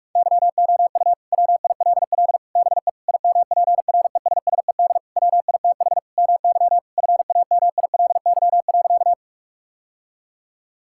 Morse code, heard as a keyed tone, about 41 wpm, 700 Hz. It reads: YOU WILL BE IMPRESSED WITH MY FAMILY.
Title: Sentences from Top 1600 Words 45wpm